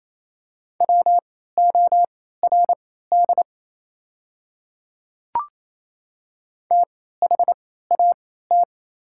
Morse code, heard as a keyed tone, about 28 wpm, spaced as for 16 wpm, 690 Hz. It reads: WORD THAT
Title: 3x Character Spacing - Top 100 Words - 28wpm